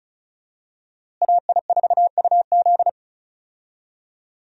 Morse code, heard as a keyed tone, about 36 wpm, 700 Hz. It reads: AI4UZ
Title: USA Call Signs - 5 Characters - Repeated 3x - 35wpm